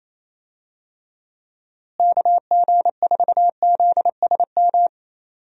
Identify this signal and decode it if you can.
Morse code, heard as a keyed tone, about 28 wpm, 700 Hz. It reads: KG4ZSM
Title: USA Call Signs - 6 Characters - Repeated 2x - 28wpm